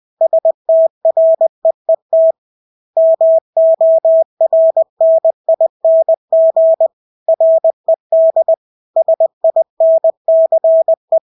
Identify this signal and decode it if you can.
Morse code, heard as a keyed tone, about 20 wpm, 650 Hz. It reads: STREET MORNING RED SINCE
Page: Engelske ord 20 wpm